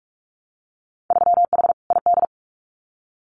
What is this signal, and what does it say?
Morse code, heard as a keyed tone, about 46 wpm, 700 Hz. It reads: 3H ID
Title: Winter Field Day Exchanges Encore 45wpm